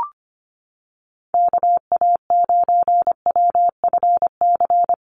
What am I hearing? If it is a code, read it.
Morse code, heard as a keyed tone, about 25 wpm, 700 Hz. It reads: KA9WFC
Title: USA Call Signs Repeated 2x 25wpm